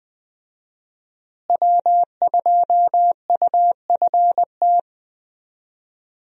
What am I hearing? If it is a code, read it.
Morse code, heard as a keyed tone, about 20 wpm, 700 Hz. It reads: W2UFT
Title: USA Call Signs - 5 Characters - Repeated 3x - 20wpm